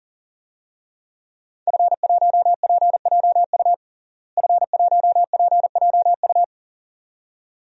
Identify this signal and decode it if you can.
Morse code, heard as a keyed tone, about 40 wpm, 700 Hz. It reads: F1PJU F1PJU
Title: DXCC-100 Call Signs - 5 Characters - Repeated 2x - 40wpm